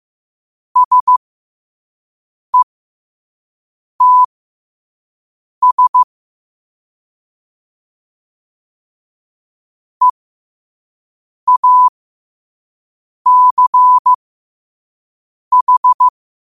Morse code, heard as a keyed tone, about 15 wpm, spaced as for 5 wpm, 1000 Hz. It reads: SETS EACH
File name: qotdfiles / 5wpm